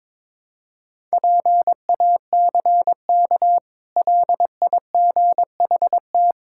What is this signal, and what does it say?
Morse code, heard as a keyed tone, about 22 wpm, 690 Hz. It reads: PACK LIGHT
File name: practice / 22wpm / Easy Sentences with Hints - 22wpm